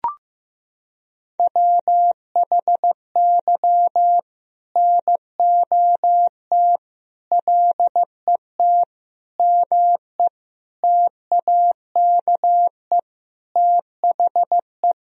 Morse code, heard as a keyed tone, about 15 wpm, 700 Hz. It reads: WHY NOT LET ME TAKE THE
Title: Sentences from Top 1000 Words - Review Part 2 - 15wpm